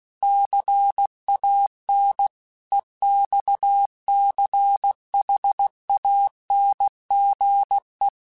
Morse code, heard as a keyed tone, about 16 wpm, 800 Hz. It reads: CAN EXCHANGE